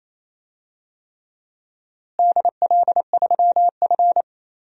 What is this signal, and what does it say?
Morse code, heard as a keyed tone, about 28 wpm, 700 Hz. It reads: DL3F
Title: DXCC-100 Call Signs - 4 Characters - Repeated 3x - 28wpm